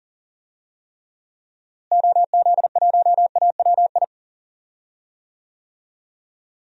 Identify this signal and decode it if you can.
Morse code, heard as a keyed tone, about 40 wpm, 700 Hz. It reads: OZ1AWI
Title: DXCC-100 Call Signs - 6 Characters - Repeated 3x - 40wpm